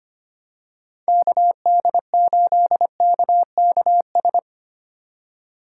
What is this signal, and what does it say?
Morse code, heard as a keyed tone, about 25 wpm, 700 Hz. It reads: KD8KKS
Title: USA Call Signs - 6 Characters - 4x Character Spacing - 25wpm